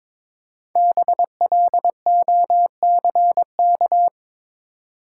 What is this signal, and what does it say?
Morse code, heard as a keyed tone, about 22 wpm, 700 Hz. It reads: BLOCK
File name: 5 Letter Words - First 3 Letters Given - 22wpm